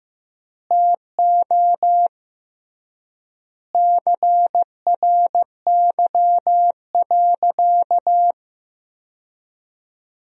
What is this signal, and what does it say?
Morse code, heard as a keyed tone, about 15 wpm, 700 Hz. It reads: TO CRY.
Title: 3x Word-Spacing - Sentences from Top 500 Words - 15wpm